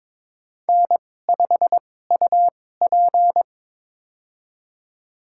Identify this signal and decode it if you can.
Morse code, heard as a keyed tone, about 22 wpm, spaced as for 16 wpm, 700 Hz. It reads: N5UP
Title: USA Call Signs - 2x Character Spacing - 22wpm